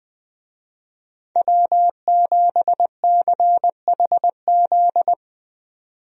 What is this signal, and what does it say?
Morse code, heard as a keyed tone, about 20 wpm, 700 Hz. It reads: W7CHZ
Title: USA Call Signs - 5 Characters - 20wpm